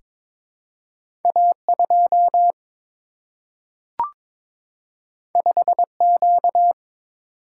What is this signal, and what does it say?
Morse code, heard as a keyed tone, about 22 wpm, 700 Hz. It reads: A 2 5 Q
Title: Single-Letter Single-Number 22wpm